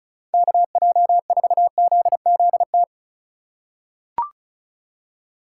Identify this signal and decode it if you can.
Morse code, heard as a keyed tone, about 34 wpm, 700 Hz. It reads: KJ4ZZT
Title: USA Call Signs - 2x Character Spacing - 35wpm